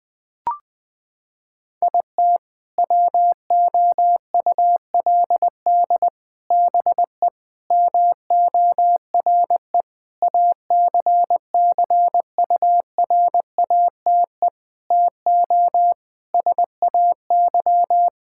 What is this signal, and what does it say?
Morse code, heard as a keyed tone, about 21 wpm, 700 Hz. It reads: IT WOULD BE MORE ACCURATE TO SAY
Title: Sentences with 8-Word Phrases - 20wpm